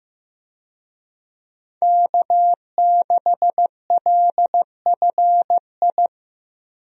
Morse code, heard as a keyed tone, about 15 wpm, 700 Hz. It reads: K6LFI
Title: USA Call Signs - 5 Characters - Repeated 3x - 15wpm